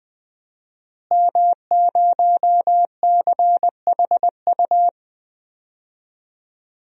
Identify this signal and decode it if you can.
Morse code, heard as a keyed tone, about 20 wpm, 700 Hz. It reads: M0CHU
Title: DXCC-100 Call Signs - 5 Characters - Repeated 2x - 20wpm